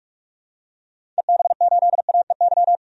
Morse code, heard as a keyed tone, about 46 wpm, 700 Hz. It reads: EB8AEY
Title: DXCC-100 Call Signs - 6 Characters - 45wpm